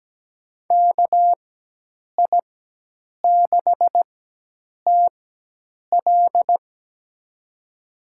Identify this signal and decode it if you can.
Morse code, heard as a keyed tone, about 17 wpm, spaced as for 8 wpm, 700 Hz. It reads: KI6TL
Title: USA Call Signs - 4x Character Spacing - 17wpm